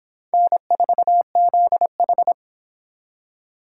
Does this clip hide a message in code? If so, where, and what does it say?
Yes: Morse code from 0.3 to 2.3 s, heard as a keyed tone, about 25 wpm, 700 Hz. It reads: N4ZH